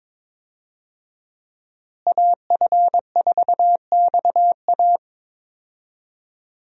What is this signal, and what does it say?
Morse code, heard as a keyed tone, about 22 wpm, 700 Hz. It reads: AF4XA
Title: USA Call Signs - 5 Characters - Repeated 2x - 22wpm